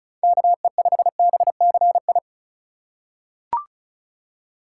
Morse code, heard as a keyed tone, about 36 wpm, 700 Hz. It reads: KE5BCI
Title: USA Call Signs - 4x Character Spacing - 35wpm